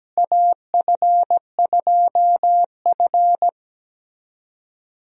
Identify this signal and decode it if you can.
Morse code, heard as a keyed tone, about 17 wpm, 700 Hz. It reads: AF2F
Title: USA Call Signs - 4 Characters - Repeated 3x - 17wpm